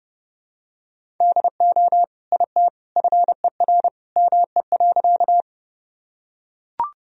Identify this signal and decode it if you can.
Morse code, heard as a keyed tone, about 30 wpm, 700 Hz. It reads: DO IT FER ME.
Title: Sentences from Top 200 Words and Abbreviations 30wpm